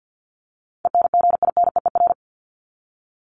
Morse code, heard as a keyed tone, about 50 wpm, 700 Hz. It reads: ENGINEER
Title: ING Suffix - Top 100 Words with Base Words 50wpm